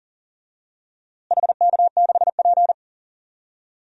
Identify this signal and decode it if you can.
Morse code, heard as a keyed tone, about 40 wpm, 700 Hz. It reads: HK6P
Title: DXCC-100 Call Signs - 4 Characters - Repeated 3x - 40wpm